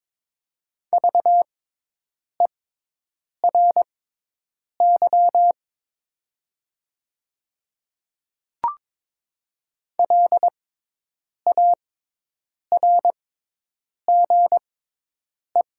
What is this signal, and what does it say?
Morse code, heard as a keyed tone, about 22 wpm, spaced as for 8 wpm, 700 Hz. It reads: VERY LARGE
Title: 6x Character Spacing - Top 300 Words - 22wpm